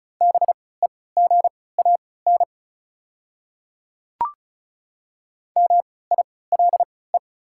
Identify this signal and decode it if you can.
Morse code, heard as a keyed tone, about 34 wpm, spaced as for 20 wpm, 700 Hz. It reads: BEGAN MILE
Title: 3x Character Spacing - Top 300 Words - 35wpm